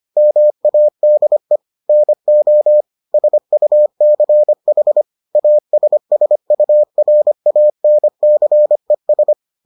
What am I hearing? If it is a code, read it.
Morse code, heard as a keyed tone, about 25 wpm, 600 Hz. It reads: MADE NO SUCH ASSURANCES